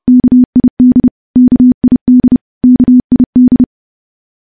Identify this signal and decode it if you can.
Morse code, heard as a keyed tone, about 30 wpm, 250 Hz. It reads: KID KID KID
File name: morse / morse-training-30WPM